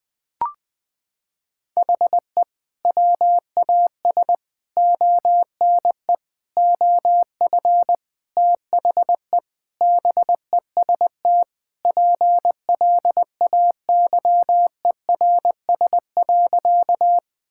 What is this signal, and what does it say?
Morse code, heard as a keyed tone, about 20 wpm, 700 Hz. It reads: HE WAS ONE OF THE BEST PLAYERS.
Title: Sentences with 5-Word Phrases - 20wpm